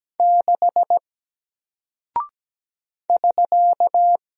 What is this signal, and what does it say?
Morse code, heard as a keyed tone, about 17 wpm, 700 Hz. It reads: 6 <SK>
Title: Single Character (Letter:Number:Punctuation:Prosign) Alt 17wpm